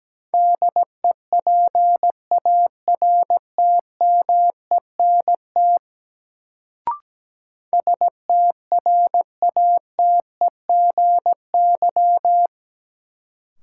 Morse code, heard as a keyed tone, about 17 wpm, 700 Hz. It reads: DEPARTMENT STRATEGY
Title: Sentences from Top 1000 Words - Review Part 5 - New Word Drill - 17wpm